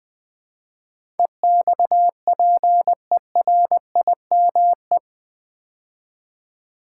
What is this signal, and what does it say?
Morse code, heard as a keyed tone, about 20 wpm, 700 Hz. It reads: EXPERIME
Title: Build A Word - 4 Letters At A Time - 12-Letter Words - 20wpm